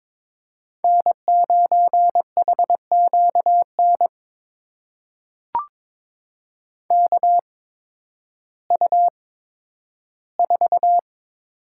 Morse code, heard as a keyed tone, about 22 wpm, spaced as for 12 wpm, 700 Hz. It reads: N9HQN KU4
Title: USA Call Signs - 5 Characters - 8x Character Spacing - 22wpm